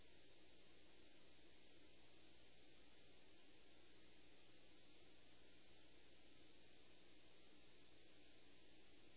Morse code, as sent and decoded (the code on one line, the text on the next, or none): none